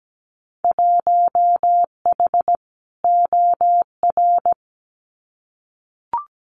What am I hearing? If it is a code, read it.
Morse code, heard as a keyed tone, about 17 wpm, 700 Hz. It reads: 1H OR
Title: Winter Field Day Exchanges 17wpm